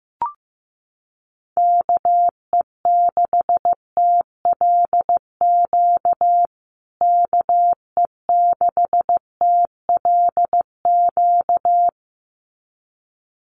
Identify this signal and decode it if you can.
Morse code, heard as a keyed tone, about 15 wpm, 700 Hz. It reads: KE6TLQ KE6TLQ
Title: USA Call Signs Repeated 2x 15wpm